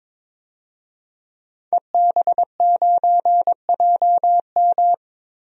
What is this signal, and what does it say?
Morse code, heard as a keyed tone, about 22 wpm, 700 Hz. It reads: EB9JM
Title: DXCC-100 Call Signs - 5 Characters - 22wpm